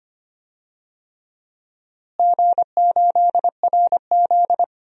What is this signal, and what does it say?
Morse code, heard as a keyed tone, about 25 wpm, 700 Hz. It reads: G8RZ
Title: DXCC-100 Call Signs - 4 Characters - Repeated 3x - 25wpm